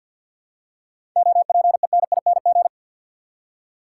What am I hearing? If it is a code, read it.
Morse code, heard as a keyed tone, about 53 wpm, 700 Hz. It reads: OPENING
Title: Seven Letter Words - 50wpm